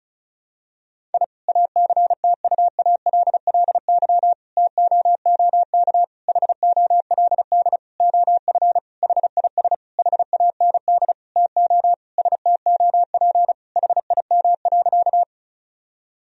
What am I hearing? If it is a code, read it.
Morse code, heard as a keyed tone, about 34 wpm, 700 Hz. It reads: I ACTUALLY TOOK HOLD OF HIS HAND TO STOP HIM.
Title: Sentences from Top 1000 Words - Review Part 4 - 35wpm